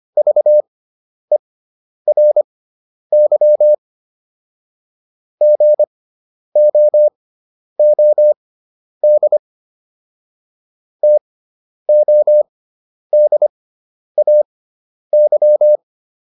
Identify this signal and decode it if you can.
Morse code, heard as a keyed tone, about 25 wpm, spaced as for 10 wpm, 600 Hz. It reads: VERY GOOD TODAY